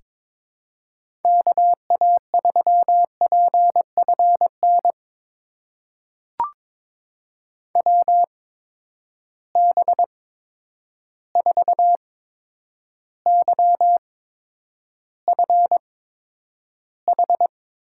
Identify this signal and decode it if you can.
Morse code, heard as a keyed tone, about 22 wpm, spaced as for 9 wpm, 700 Hz. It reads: KA3PFN WB4YFH
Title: USA Call Signs - 6 Characters - 8x Character Spacing - 22wpm